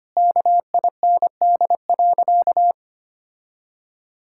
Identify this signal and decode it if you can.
Morse code, heard as a keyed tone, about 25 wpm, 700 Hz. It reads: KIND.
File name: practice / 25wpm / Sentences with 2-Word Phrases - 25wpm